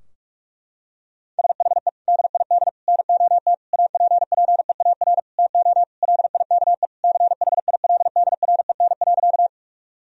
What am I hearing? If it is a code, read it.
Morse code, heard as a keyed tone, about 42 wpm, 700 Hz. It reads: SHE DID NOT APPEAR TO LIKE CHILDREN.
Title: Sentences from Top 1000 Words - Review Part 2 - 45wpm